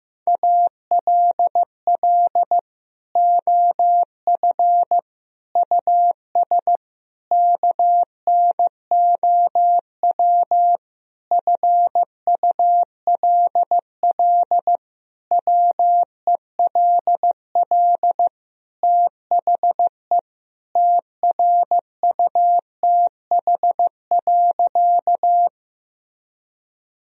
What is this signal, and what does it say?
Morse code, heard as a keyed tone, about 15 wpm, 700 Hz. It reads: ALL OF US KNOW FULL WELL THE TRUTH.
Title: Sentences from Top 1000 Words - Review Part 3 - 15wpm